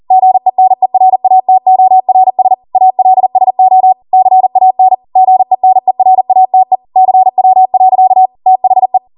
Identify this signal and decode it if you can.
Morse code, heard as a keyed tone, about 40 wpm, 750 Hz. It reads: GENERATORS ALSO CAN GENERATE CW. THE